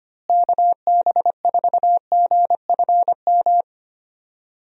Morse code, heard as a keyed tone, about 25 wpm, 700 Hz. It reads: KB4GFM
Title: USA Call Signs - 6 Characters - Repeated 2x - 25wpm